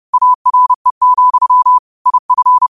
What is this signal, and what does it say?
Morse code, heard as a keyed tone, about 30 wpm, 1000 Hz. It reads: ARE, IF